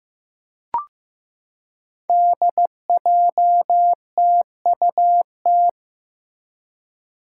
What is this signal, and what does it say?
Morse code, heard as a keyed tone, about 15 wpm, 700 Hz. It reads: DJTUT
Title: Five Challenging Letter-Number Combinations 15wpm